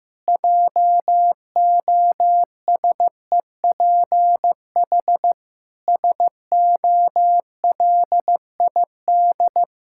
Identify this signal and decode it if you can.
Morse code, heard as a keyed tone, about 15 wpm, 700 Hz. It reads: JOSEPH SOLID
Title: Practice QSO - Exchange 3 - More Info - 15wpm